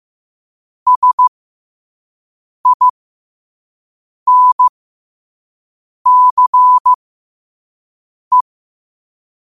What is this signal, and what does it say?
Morse code, heard as a keyed tone, about 15 wpm, spaced as for 5 wpm, 1000 Hz. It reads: SINCE
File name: qotdfiles / 5wpm